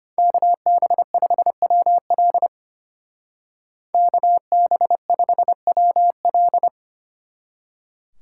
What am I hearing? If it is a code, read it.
Morse code, heard as a keyed tone, about 33 wpm, 700 Hz. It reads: KB5WL KB5WL